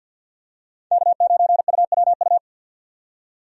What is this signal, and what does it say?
Morse code, heard as a keyed tone, about 50 wpm, 700 Hz. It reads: K9UWU
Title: USA Call Signs - 5 Characters - 50wpm